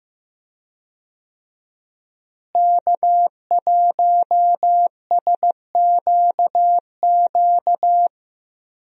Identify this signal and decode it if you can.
Morse code, heard as a keyed tone, about 15 wpm, 700 Hz. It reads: K1SQQ
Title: USA Call Signs - 5 Characters - Repeated 3x - 15wpm